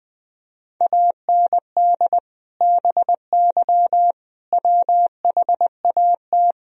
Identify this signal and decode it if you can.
Morse code, heard as a keyed tone, about 21 wpm, 700 Hz. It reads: AND BY WHAT
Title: Sentences from Top 1000 Words - Review Part 3 - 20wpm